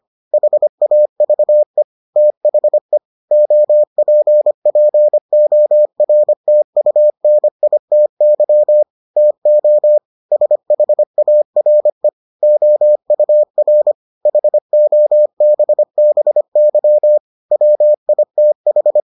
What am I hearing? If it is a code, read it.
Morse code, heard as a keyed tone, about 25 wpm, 600 Hz. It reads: HAVE THE OPPORTUNITY TO SHARE OUR HOBBY WITH